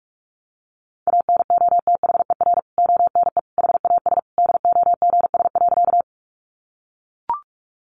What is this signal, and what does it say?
Morse code, heard as a keyed tone, about 44 wpm, 700 Hz. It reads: ANOTHER ONE HAS DOGS.
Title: Additional Easy Phrases 45wpm